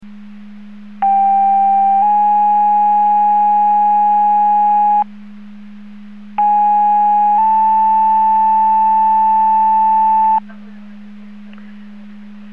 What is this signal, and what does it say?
Two-tone page, heard as a one-second tone, then a three-second tone, about 790 then 830 Hz, about 830 then 880 Hz. It is heard from 1.0 to 10.4 s.